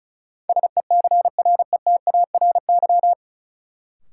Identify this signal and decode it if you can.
Morse code, heard as a keyed tone, about 34 wpm, 700 Hz. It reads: SECRETARY
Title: Sentences from Top 1300 Words - New Word Drill Alt - 35wpm